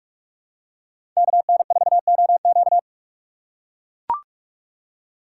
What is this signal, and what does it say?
Morse code, heard as a keyed tone, about 41 wpm, 700 Hz. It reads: KN4OQ
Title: USA Call Signs - 5 Characters - 4x Character Spacing - 45wpm